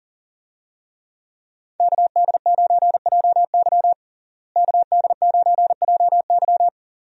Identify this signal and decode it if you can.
Morse code, heard as a keyed tone, about 40 wpm, 700 Hz. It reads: KD9JY KD9JY
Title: USA Call Signs - 5 Characters - Repeated 2x - 40wpm